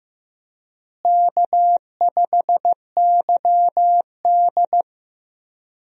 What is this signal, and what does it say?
Morse code, heard as a keyed tone, about 15 wpm, 700 Hz. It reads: K5YD
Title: USA Call Signs - 4 Characters - Repeated 3x - 15wpm